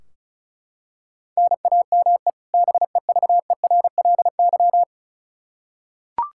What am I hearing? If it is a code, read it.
Morse code, heard as a keyed tone, about 36 wpm, 700 Hz. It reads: NAME BEVERLY
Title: Practice QSO - Exchange 1 - Beginning - 35wpm